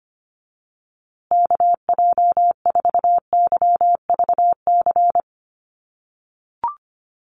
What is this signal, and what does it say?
Morse code, heard as a keyed tone, about 25 wpm, 700 Hz. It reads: KJ4YVC
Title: USA Call Signs Repeated 2x 25wpm